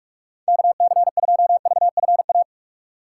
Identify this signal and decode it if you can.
Morse code, heard as a keyed tone, about 45 wpm, 700 Hz. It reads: KC2VFA
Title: USA Call Signs - 6 Characters - 2x Character Spacing - 45wpm